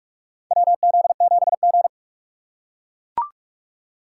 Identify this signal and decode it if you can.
Morse code, heard as a keyed tone, about 43 wpm, 700 Hz. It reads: WZ7G
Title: USA Call Signs - 4 Characters - 6x Character Spacing - 45wpm